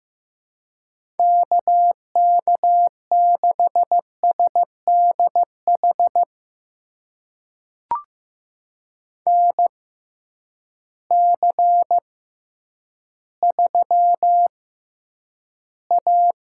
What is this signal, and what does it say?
Morse code, heard as a keyed tone, about 15 wpm, spaced as for 9 wpm, 700 Hz. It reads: KK6SDH NC3A
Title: USA Call Signs - 6x Character Spacing - 15wpm